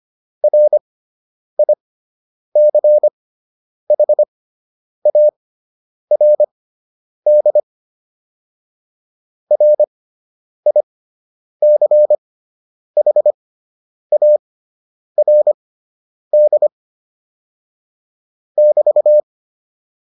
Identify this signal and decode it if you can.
Morse code, heard as a keyed tone, about 25 wpm, spaced as for 9 wpm, 600 Hz. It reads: RICHARD RICHARD <BT>